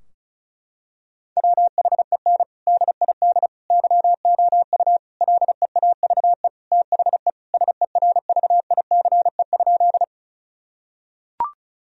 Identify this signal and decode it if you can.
Morse code, heard as a keyed tone, about 36 wpm, 700 Hz. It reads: WHEN DID YOU LEAVE THE SERVICE?
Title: Sentences from Top 1000 Words - Review Part 4 - 35wpm